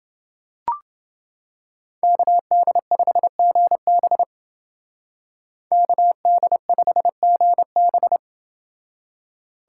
Morse code, heard as a keyed tone, about 28 wpm, 700 Hz. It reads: KD5GB KD5GB